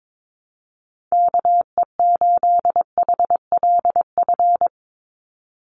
Morse code, heard as a keyed tone, about 22 wpm, 700 Hz. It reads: KE8HLF
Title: USA Call Signs 22wpm